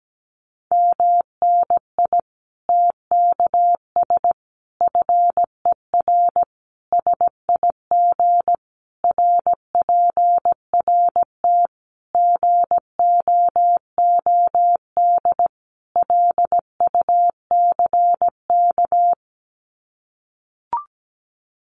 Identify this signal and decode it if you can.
Morse code, heard as a keyed tone, about 17 wpm, 700 Hz. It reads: MNI TKS FER SIG RPRT GOOD LUCK
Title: Common QSO Phrases 17wpm